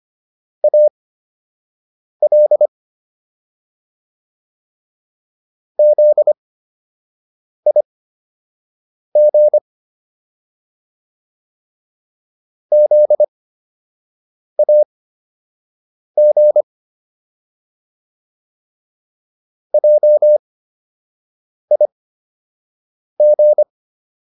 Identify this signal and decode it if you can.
Morse code, heard as a keyed tone, about 25 wpm, spaced as for 6 wpm, 600 Hz. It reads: AL ZIG ZAG JIG